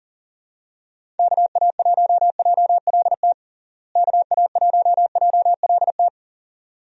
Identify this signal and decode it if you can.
Morse code, heard as a keyed tone, about 40 wpm, 700 Hz. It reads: KA1JLT KA1JLT
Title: USA Call Signs - 6 Characters - Repeated 2x - 40wpm